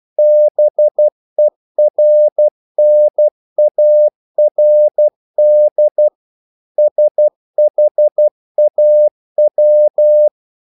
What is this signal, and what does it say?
Morse code, heard as a keyed tone, about 12 wpm, 600 Hz. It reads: BERNARD SHAW
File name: CW / Practice / EnglishQuotations0112